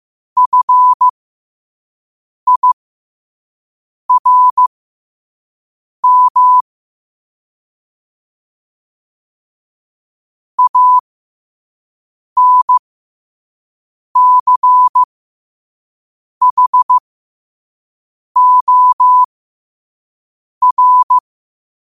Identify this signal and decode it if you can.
Morse code, heard as a keyed tone, about 15 wpm, spaced as for 5 wpm, 1000 Hz. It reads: FIRM ANCHOR
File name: 2026-03-16-5wpm